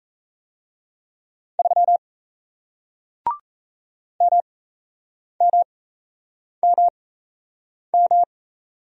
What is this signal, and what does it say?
Morse code, heard as a keyed tone, about 37 wpm, spaced as for 8 wpm, 700 Hz. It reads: 3 MMMM